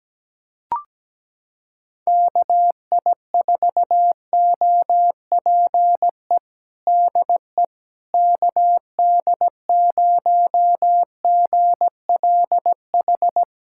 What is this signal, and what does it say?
Morse code, heard as a keyed tone, about 17 wpm, 700 Hz. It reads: KI4OPE DE KD0GLH
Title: USA Callsign DE Callsign 17wpm